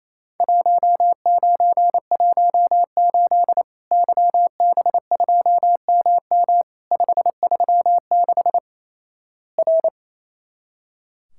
Morse code, heard as a keyed tone, about 28 wpm, 700 Hz. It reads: 1918 YB2MM 536 R
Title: Worked All Europe DX Contest - QTC Exchange - 28wpm